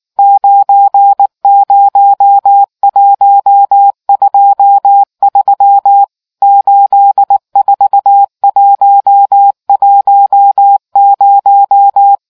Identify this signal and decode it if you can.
Morse code, heard as a keyed tone, about 19 wpm, 800 Hz. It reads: 90123 84110